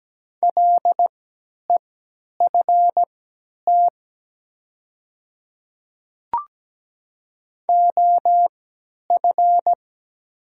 Morse code, heard as a keyed tone, about 17 wpm, spaced as for 10 wpm, 700 Hz. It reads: LEFT OF
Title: 3x Character Spacing - Top 300 Words - 17wpm